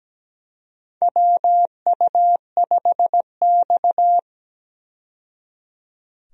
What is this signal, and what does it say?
Morse code, heard as a keyed tone, about 17 wpm, 700 Hz. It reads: WU5X
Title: USA Call Signs - 4 Characters - 17wpm